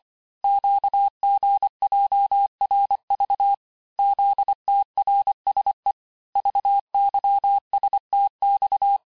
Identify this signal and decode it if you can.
Morse code, heard as a keyed tone, about 25 wpm, 800 Hz. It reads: QGJRV ZTRSE VYSTX
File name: texte-chiffre-vitesse-23mpm